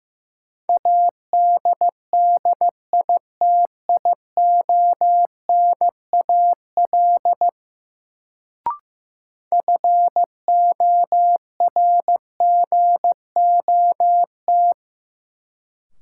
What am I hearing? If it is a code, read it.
Morse code, heard as a keyed tone, about 15 wpm, 700 Hz. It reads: ADDITIONAL FORGOT
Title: Sentences from Top 1400 Words - New Word Drill Alt - 15wpm